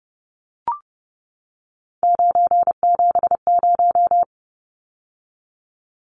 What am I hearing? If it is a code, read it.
Morse code, heard as a keyed tone, about 30 wpm, 700 Hz. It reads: 970
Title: Numbers - 3-Digit 30wpm